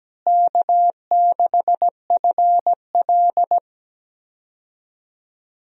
Morse code, heard as a keyed tone, about 17 wpm, 700 Hz. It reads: K6FL
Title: USA Call Signs - 4 Characters - Repeated 3x - 17wpm